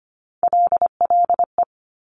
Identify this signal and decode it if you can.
Morse code, heard as a keyed tone, about 25 wpm, 700 Hz. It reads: LLE
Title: Top Three Letter Combinations in Words 25wpm